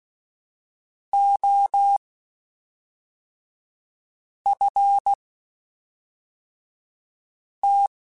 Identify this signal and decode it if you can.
Morse code, heard as a keyed tone, about 16 wpm, 800 Hz. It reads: O F T
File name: (spaced) / QSOs Spaced-mp3s / kc7haa @5 sp